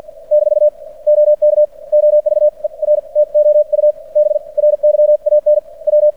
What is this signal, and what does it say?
Morse code, heard as a keyed tone, about 55 wpm, spaced as for 42 wpm, 600 Hz. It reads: - OK OVER TO U BRYAN W
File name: archived / CRSnet